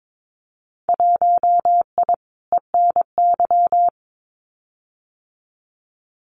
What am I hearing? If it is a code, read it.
Morse code, heard as a keyed tone, about 22 wpm, 700 Hz. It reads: 1I ENY
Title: Winter Field Day Exchanges Encore 22wpm